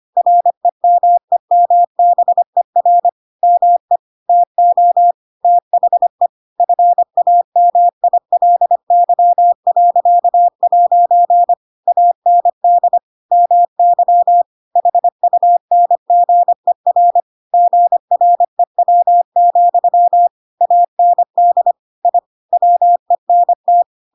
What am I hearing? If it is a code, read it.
Morse code, heard as a keyed tone, about 25 wpm, 700 Hz. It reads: REMEMBER ME TO THE FAMILY.' AND MY HUNGER GREW, AND I WENT
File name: Pinocchio / Pinocchio ch7 25 25 ik1hge